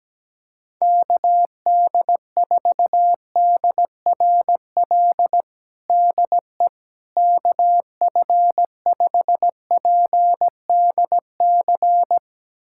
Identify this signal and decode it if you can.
Morse code, heard as a keyed tone, about 17 wpm, 700 Hz. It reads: KD4DRL DE KF5PDC
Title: USA Callsign DE Callsign - 6 Characters - 17wpm